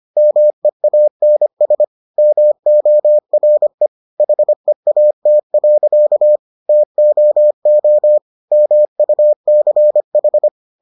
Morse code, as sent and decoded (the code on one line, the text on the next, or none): -- . .- -. ... / -- --- .-. / .... .- - .-.-.- / - --- --- / -- ..- -.-. ....
MEANS MORE HEAT. TOO MUCH